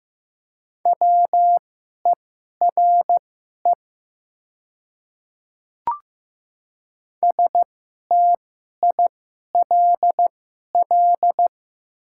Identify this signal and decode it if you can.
Morse code, heard as a keyed tone, about 15 wpm, spaced as for 11 wpm, 700 Hz. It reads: WERE STILL
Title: 2x Character Spacing - Top 500 Words - 15wpm